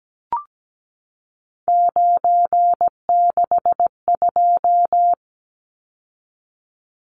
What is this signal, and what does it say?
Morse code, heard as a keyed tone, about 17 wpm, 700 Hz. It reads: 962
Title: Numbers - 3-Digit 17wpm